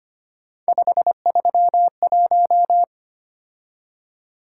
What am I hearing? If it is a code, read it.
Morse code, heard as a keyed tone, about 25 wpm, 700 Hz. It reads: 531
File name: Three Letter-Number Combinations - 25wpm